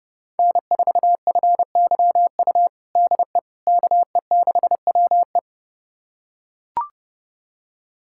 Morse code, heard as a keyed tone, about 31 wpm, 700 Hz. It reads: N4FYU DE KE6WE
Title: USA Callsign DE Callsign - 5 Characters - 30wpm